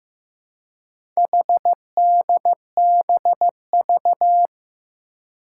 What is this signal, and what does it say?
Morse code, heard as a keyed tone, about 15 wpm, 700 Hz. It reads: HDBV
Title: Four Challenging Letter-Number Combinations 15wpm